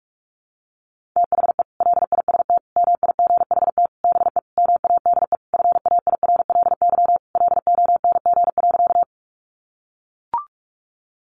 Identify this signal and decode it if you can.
Morse code, heard as a keyed tone, about 44 wpm, 700 Hz. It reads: THE LIST MIGHT BE MADE FAIRLY LONG.
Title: Sentences from Top 1100 Words 45wpm